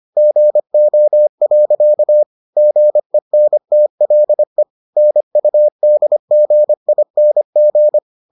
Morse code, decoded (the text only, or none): GO. GENTLE NUDGING